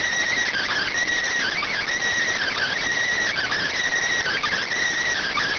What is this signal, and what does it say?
Emergency Alert System (SAME) tones (header bursts), heard as harsh buzzing bursts. It is heard from 0.4 to 4.7 s.